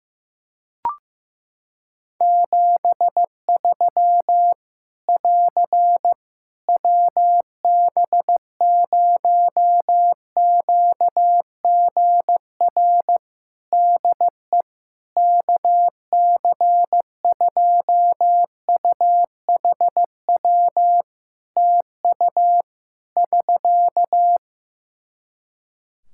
Morse code, heard as a keyed tone, about 15 wpm, 700 Hz. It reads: 73 <AR> WB0QGR DE KC2UHW TU <SK>
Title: Practice QSO - Exchange 4 - Ending - 15wpm